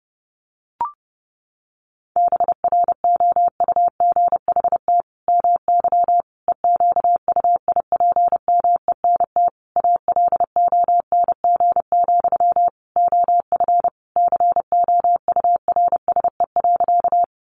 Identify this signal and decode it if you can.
Morse code, heard as a keyed tone, about 28 wpm, 700 Hz. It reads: BROUGHT MY EQUIPMENT ALONG, OF COURSE.
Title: Sentences from Top 700 Words 30wpm